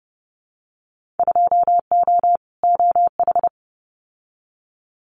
Morse code, heard as a keyed tone, about 30 wpm, 700 Hz. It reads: 2O OH
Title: Winter Field Day Exchanges 30wpm